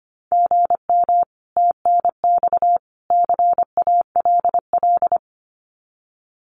Morse code, heard as a keyed tone, about 25 wpm, 700 Hz. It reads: GM TNX CALL
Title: Common QSO Phrases 25wpm